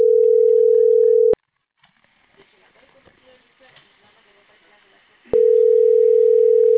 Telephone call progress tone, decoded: ringback tone

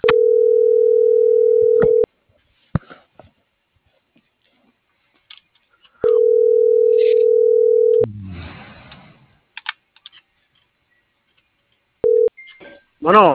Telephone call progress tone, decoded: ringback tone

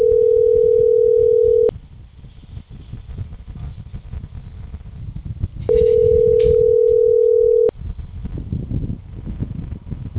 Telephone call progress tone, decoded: ringback tone

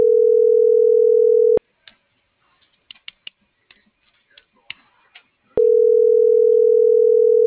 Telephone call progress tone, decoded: ringback tone

